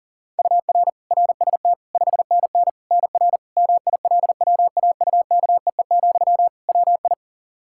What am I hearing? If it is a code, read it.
Morse code, heard as a keyed tone, about 40 wpm, 700 Hz. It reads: UR RST 5NN NR MILWAUKEE, WI
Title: Practice QSO - Exchange 1 - Beginning - 40wpm